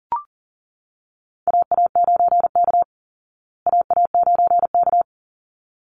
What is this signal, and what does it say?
Morse code, heard as a keyed tone, about 40 wpm, 700 Hz. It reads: AA9K AA9K
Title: Call Signs Repeated 3x - 40wpm